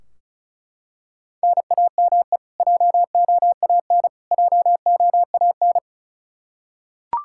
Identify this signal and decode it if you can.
Morse code, heard as a keyed tone, about 34 wpm, 700 Hz. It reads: NAME JOAN JOAN
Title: Practice QSO - Exchange 1 - Beginning - 35wpm